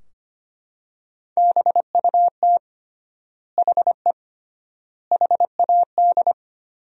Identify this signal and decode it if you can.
Morse code, heard as a keyed tone, about 25 wpm, 700 Hz. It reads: BUT HE HAD
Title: 3x Word-Spacing - Sentences from Top 100 Words - 25wpm